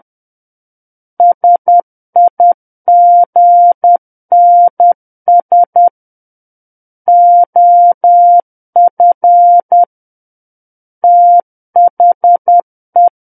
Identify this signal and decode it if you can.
Morse code, heard as a keyed tone, about 10 wpm, 700 Hz. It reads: SIGNS OF THE
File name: Book One / 01-Chapter 1 - 10 WPM